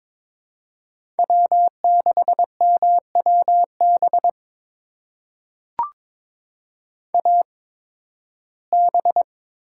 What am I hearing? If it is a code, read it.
Morse code, heard as a keyed tone, about 21 wpm, spaced as for 14 wpm, 700 Hz. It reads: W6MWB AB
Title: USA Call Signs - 5 Characters - 8x Character Spacing - 22wpm